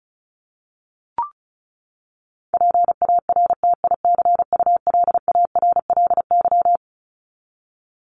Morse code, heard as a keyed tone, about 34 wpm, 700 Hz. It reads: PARTICULARLY
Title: LY Suffix - Top 100 Words with Base Words 35wpm